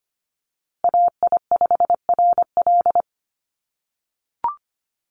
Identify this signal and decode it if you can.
Morse code, heard as a keyed tone, about 25 wpm, 700 Hz. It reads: AI5RL
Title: USA Call Signs Repeated 2x 25wpm